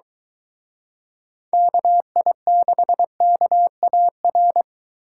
Morse code, heard as a keyed tone, about 23 wpm, 700 Hz. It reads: KI6KAR